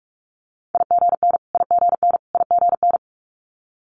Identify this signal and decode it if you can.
Morse code, heard as a keyed tone, about 46 wpm, 690 Hz. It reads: IGN IGN IGN
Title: Multi-letter Phonemes Repeated 3x 45wpm